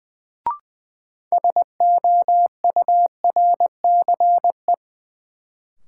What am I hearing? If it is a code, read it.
Morse code, heard as a keyed tone, about 21 wpm, 700 Hz. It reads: SOURCE
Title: Sentences from Top 800 Words - New Word Drill - 20wpm